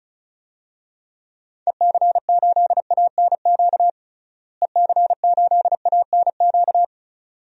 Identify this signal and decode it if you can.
Morse code, heard as a keyed tone, about 36 wpm, 700 Hz. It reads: EC8ANQ EC8ANQ
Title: DXCC-100 Call Signs - 6 Characters - Repeated 2x - 35wpm